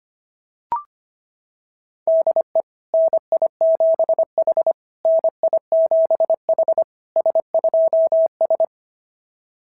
Morse code, heard as a keyed tone, about 25 wpm, 690 Hz. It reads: DE NI7H NI7H S2S
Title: SOTA QSO Practice 25wpm